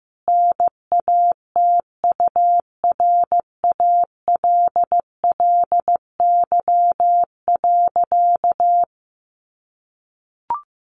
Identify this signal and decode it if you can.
Morse code, heard as a keyed tone, about 15 wpm, 700 Hz. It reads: NATURALLY.
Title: Sentences from Top 1300 Words 15wpm